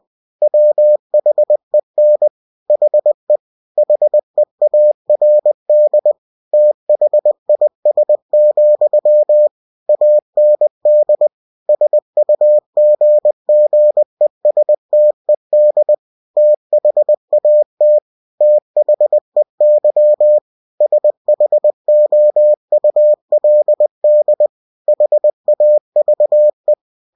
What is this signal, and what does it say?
Morse code, heard as a keyed tone, about 20 wpm, 600 Hz. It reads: WHEN HE HEARD THIS, AND SUGGESTED THAT THEY SHOULD HAVE
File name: cw-books / winnie-the-pooh / winnie-20wpm-page-0093